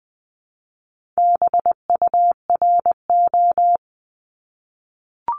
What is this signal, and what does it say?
Morse code, heard as a keyed tone, about 20 wpm, 700 Hz. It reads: BURO
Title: 260 Lesson 42e - Words 20-20wpm